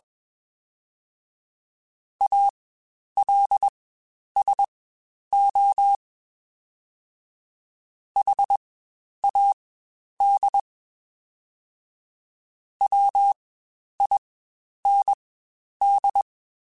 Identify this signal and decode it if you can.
Morse code, heard as a keyed tone, about 21 wpm, spaced as for 10 wpm, 800 Hz. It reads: ALSO HAD WIND
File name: Spaced Characters / Spaced Characters-mp3s / Weather5 5 sp @20